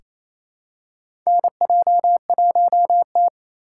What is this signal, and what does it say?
Morse code, heard as a keyed tone, about 28 wpm, 700 Hz. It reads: NJ1T
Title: K1USN SST QSO Practice - 28wpm